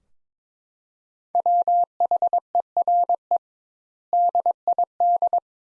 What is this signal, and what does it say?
Morse code, heard as a keyed tone, about 22 wpm, 700 Hz. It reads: WHERE DID
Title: Sentences Top 200 Words - 22wpm